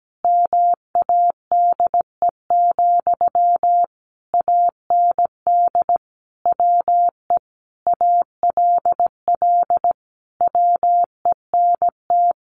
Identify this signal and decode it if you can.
Morse code, heard as a keyed tone, about 17 wpm, 700 Hz. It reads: MADE, AND WE ALL WENT
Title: Sentences from Top 1400 Words 17wpm